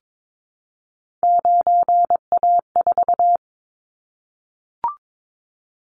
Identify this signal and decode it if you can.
Morse code, heard as a keyed tone, about 22 wpm, 700 Hz. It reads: 9A4
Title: Callsign Prefixes for Top 50 DXCC - 22wpm